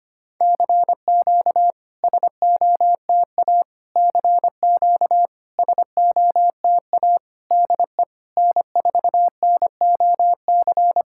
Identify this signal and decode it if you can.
Morse code, heard as a keyed tone, about 25 wpm, 690 Hz. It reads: CQ SOTA CQ SOTA DE N4NOC
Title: SOTA QSO Practice 25wpm